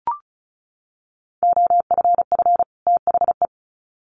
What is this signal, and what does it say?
Morse code, heard as a keyed tone, about 34 wpm, 700 Hz. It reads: OFF THE
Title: Sets of 2 Words 35wpm